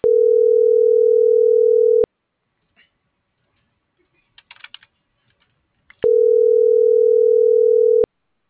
Ringback tone (North America), 440 + 480 Hz.